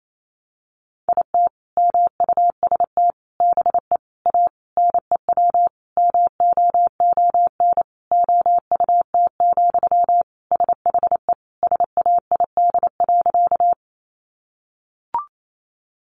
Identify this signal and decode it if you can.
Morse code, heard as a keyed tone, about 28 wpm, 700 Hz. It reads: IT MUST BE A NEW MOON OUT, SHE SAID.
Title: Sentences from Top 500 Words 28wpm